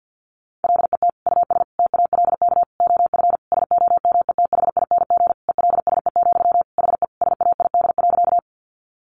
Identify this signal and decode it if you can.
Morse code, heard as a keyed tone, about 50 wpm, 700 Hz. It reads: LET US TALK OF SOMETHING ELSE, HE SAID.
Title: Sentences from Top 800 Words 50wpm